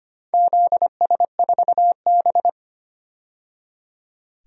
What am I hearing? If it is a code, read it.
Morse code, heard as a keyed tone, about 25 wpm, 700 Hz. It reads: ZS4B
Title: DXCC-100 Call Signs - 4 Characters - 25wpm